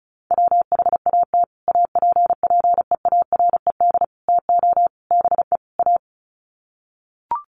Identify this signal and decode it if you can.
Morse code, heard as a keyed tone, about 34 wpm, 700 Hz. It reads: WHAT APPEARED TO BE A
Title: Sets of 5 Words Encore2 35wpm